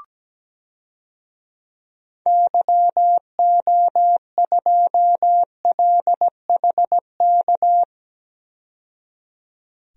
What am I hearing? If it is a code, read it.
Morse code, heard as a keyed tone, about 17 wpm, 700 Hz. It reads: YO2LHK
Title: DXCC-100 Call Signs - 6 Characters - 17wpm